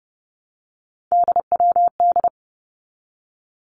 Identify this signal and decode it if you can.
Morse code, heard as a keyed tone, about 30 wpm, 700 Hz. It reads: DWD
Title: Three Challenging Letter-Number Combinations 30wpm